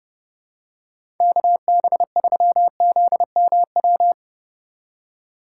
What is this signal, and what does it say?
Morse code, heard as a keyed tone, about 31 wpm, 700 Hz. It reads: KB3ZMW
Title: USA Call Signs - 6 Characters - 4x Character Spacing - 30wpm